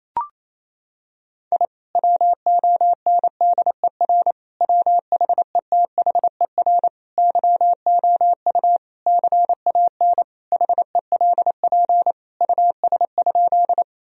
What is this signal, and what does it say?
Morse code, heard as a keyed tone, about 28 wpm, 700 Hz. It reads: I WONDER WHETHER YOU CAN HELP US?
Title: Sentences from Top 1000 Words - Review Part 3 - 28wpm